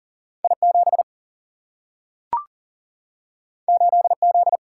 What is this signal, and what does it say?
Morse code, heard as a keyed tone, about 40 wpm, 700 Hz. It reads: I 7 8 Z
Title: Single-Letter Single-Number 40wpm